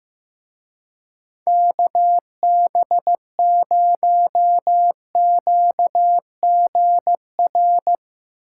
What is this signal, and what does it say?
Morse code, heard as a keyed tone, about 15 wpm, 700 Hz. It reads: KB0QGR
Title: Speed Racing - USA Call Signs 15wpm